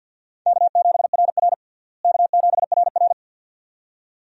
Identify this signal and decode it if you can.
Morse code, heard as a keyed tone, about 53 wpm, 700 Hz. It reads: K7RR K7RR
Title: USA Call Signs - 4 Characters - Repeated 2x - 50wpm